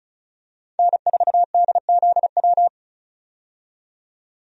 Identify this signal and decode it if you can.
Morse code, heard as a keyed tone, about 36 wpm, 700 Hz. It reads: N4DZW
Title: USA Call Signs - 5 Characters - 35wpm